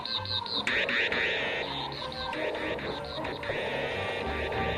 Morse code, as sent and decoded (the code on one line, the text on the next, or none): none